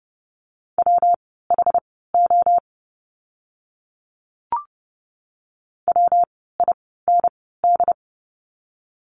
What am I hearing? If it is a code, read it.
Morse code, heard as a keyed tone, about 30 wpm, spaced as for 17 wpm, 700 Hz. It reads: WHO WIND
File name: course / 30:10wpm / 115 Lesson 19f - Letter w - Words 30wpm